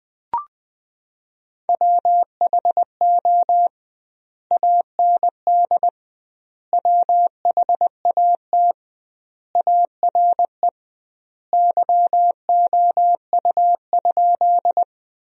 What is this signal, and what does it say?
Morse code, heard as a keyed tone, about 20 wpm, 700 Hz. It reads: WHO AND WHAT ARE YOU?
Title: 2x Word-Spacing - Sentences from Top 100 Words - 20wpm